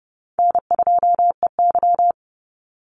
Morse code, heard as a keyed tone, about 30 wpm, 700 Hz. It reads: N2EY
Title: Easy Call Signs Repeated 3x - 30wpm